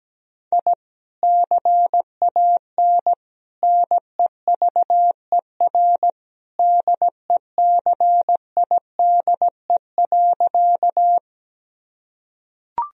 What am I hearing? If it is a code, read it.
Morse code, heard as a keyed tone, about 17 wpm, 700 Hz. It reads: I CAN NEVER DECIDE.
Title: Sentences from Top 1000 Words - Review Part 3 - 17wpm